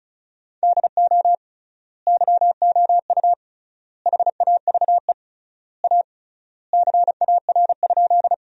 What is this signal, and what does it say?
Morse code, heard as a keyed tone, about 36 wpm, 700 Hz. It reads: DO YOU HAVE A CAR?
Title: 3x Word-Spacing - Sentences from Top 300 Words - 35wpm